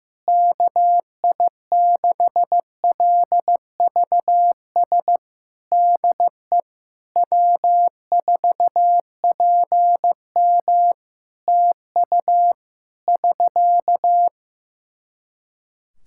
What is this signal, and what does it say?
Morse code, heard as a keyed tone, about 15 wpm, 700 Hz. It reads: KI6LVS DE W4PM TU <SK>
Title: Practice QSO - Exchange 4 - Ending - 15wpm